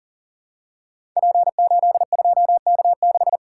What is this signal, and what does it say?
Morse code, heard as a keyed tone, about 39 wpm, 700 Hz. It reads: P82K6
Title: Code Groups - 5 Letter:Numbers - 40wpm